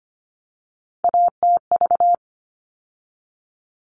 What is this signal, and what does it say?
Morse code, heard as a keyed tone, about 25 wpm, 700 Hz. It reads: ATV
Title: All Cut Numbers - 3-Digit 25wpm